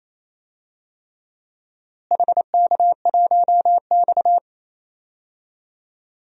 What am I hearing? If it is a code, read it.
Morse code, heard as a keyed tone, about 28 wpm, 700 Hz. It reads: HK1X
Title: DXCC-100 Call Signs - 4 Characters - Repeated 3x - 28wpm